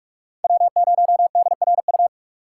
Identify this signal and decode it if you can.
Morse code, heard as a keyed tone, about 44 wpm, 700 Hz. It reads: W0DRU
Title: USA Call Signs - 5 Characters - 4x Character Spacing - 45wpm